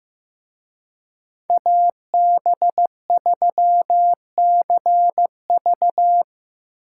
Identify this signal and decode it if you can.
Morse code, heard as a keyed tone, about 18 wpm, 700 Hz. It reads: AB3CV
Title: Speed Racing - USA 2x2 Call Signs 15wpm